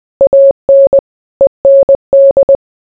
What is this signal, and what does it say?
Morse code, heard as a keyed tone, about 20 wpm, 550 Hz. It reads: AN END